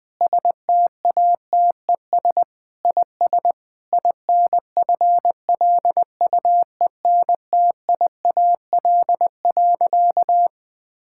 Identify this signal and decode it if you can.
Morse code, heard as a keyed tone, about 21 wpm, 700 Hz. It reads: STATES IS INFLUENTIAL.
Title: Sentences with 4-Word Phrases - Spoken Beginning - 20wpm